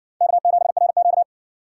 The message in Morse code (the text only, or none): K7RQ